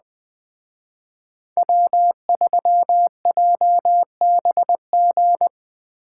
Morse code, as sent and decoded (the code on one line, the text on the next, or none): .-- ...-- .--- -... --.
W3JBG